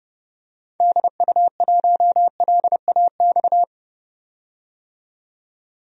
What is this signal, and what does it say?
Morse code, heard as a keyed tone, about 29 wpm, 700 Hz. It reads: DU1LAX
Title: DXCC-100 Call Signs - 6 Characters - 30wpm